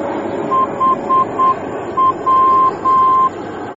Morse code, heard as a keyed tone, about 8 wpm, 1050 Hz, with noise. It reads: HW